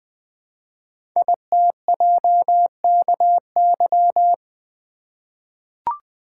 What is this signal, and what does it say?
Morse code, heard as a keyed tone, about 20 wpm, 700 Hz. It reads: ITJKY
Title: Code Groups - 5 Letters - 20wpm